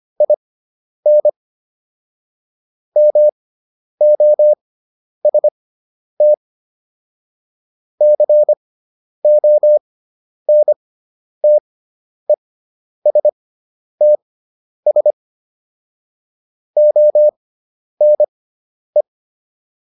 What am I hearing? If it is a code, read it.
Morse code, heard as a keyed tone, about 25 wpm, spaced as for 10 wpm, 600 Hz. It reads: IN MOST CONTESTS ONE